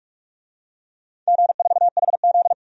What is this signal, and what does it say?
Morse code, heard as a keyed tone, about 47 wpm, 700 Hz. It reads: G4HZ
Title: DXCC-100 Call Signs - 4 Characters - 45wpm